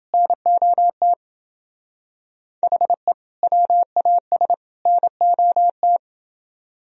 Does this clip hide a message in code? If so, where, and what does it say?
Yes: Morse code from 0.1 to 6.0 s, heard as a keyed tone, about 26 wpm, 700 Hz. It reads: NOT HE WAS NOT